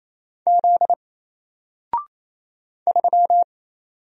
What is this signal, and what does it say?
Morse code, heard as a keyed tone, about 28 wpm, 700 Hz. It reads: Z 3
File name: Single-Letter-Number-28wpm